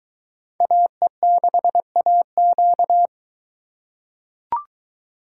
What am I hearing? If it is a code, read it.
Morse code, heard as a keyed tone, about 21 wpm, 700 Hz. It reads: AE6AQ